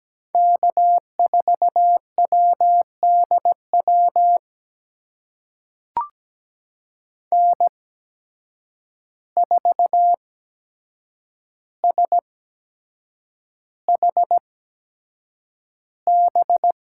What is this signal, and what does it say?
Morse code, heard as a keyed tone, about 17 wpm, spaced as for 7 wpm, 700 Hz. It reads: K4WDW N4SHB
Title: USA Call Signs - 5 Characters - 8x Character Spacing - 17wpm